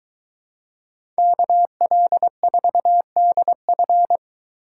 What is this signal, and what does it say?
Morse code, heard as a keyed tone, about 22 wpm, 700 Hz. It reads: KL4DF